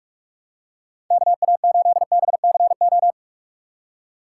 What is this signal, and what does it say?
Morse code, heard as a keyed tone, about 46 wpm, 700 Hz. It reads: K A 8 B C O
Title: USA Call Signs - 6 Characters - 6x Character Spacing - 45wpm